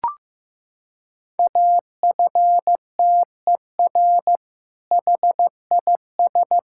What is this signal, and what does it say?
Morse code, heard as a keyed tone, about 15 wpm, 700 Hz. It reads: AFTER HIS
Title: Sentences from Top 1000 Words - Review Part 6 - 15wpm